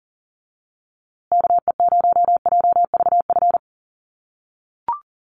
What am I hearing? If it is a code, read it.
Morse code, heard as a keyed tone, about 40 wpm, 700 Hz. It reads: KE0JVF
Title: USA Call Signs Repeated 2x 40wpm